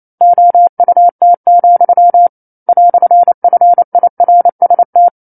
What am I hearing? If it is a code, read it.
Morse code, heard as a keyed tone, about 28 wpm, 700 Hz. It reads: OUT, "FIRST
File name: 09-Chapter-09 28 WPM